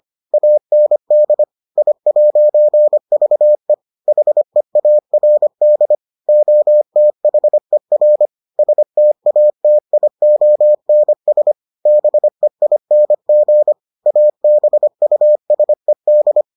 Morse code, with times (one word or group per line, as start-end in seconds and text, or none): 0.3-1.4 AND
1.8-3.7 I'VE
4.1-6.0 HEARD
6.3-8.3 OTHER
8.6-11.5 STATIONS
11.9-13.7 BEING
14.1-16.4 ABUSED